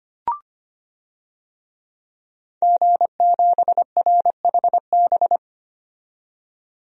Morse code, heard as a keyed tone, about 25 wpm, 690 Hz. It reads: G7RHB
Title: DXCC-100 Call Signs - 5 Characters - 25wpm